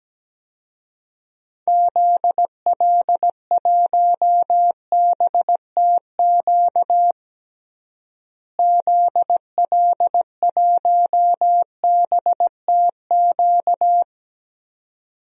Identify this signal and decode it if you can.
Morse code, heard as a keyed tone, about 17 wpm, 700 Hz. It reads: ZL1BTQ ZL1BTQ
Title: DXCC-100 Call Signs - 6 Characters - Repeated 2x - 17wpm